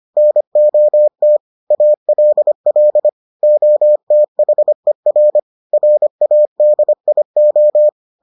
Morse code, heard as a keyed tone, about 25 wpm, 600 Hz. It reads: NOT ALL OTHER RADIO